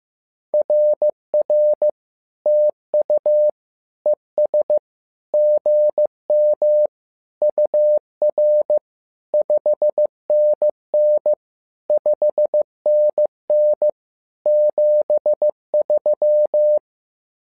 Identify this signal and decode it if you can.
Morse code, heard as a keyed tone, about 15 wpm, 600 Hz. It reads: RR TU ES GM UR 5NN 5NN 73
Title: SOTA QSO Practice 15wpm